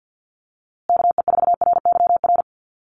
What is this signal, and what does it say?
Morse code, heard as a keyed tone, about 50 wpm, 700 Hz. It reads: KE4RYR
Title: USA Call Signs Repeated 3x 50wpm